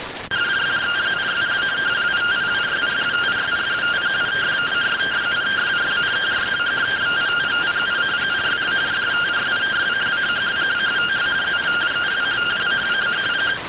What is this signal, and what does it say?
Two-tone teleprinter signal, about 170 Hz shift.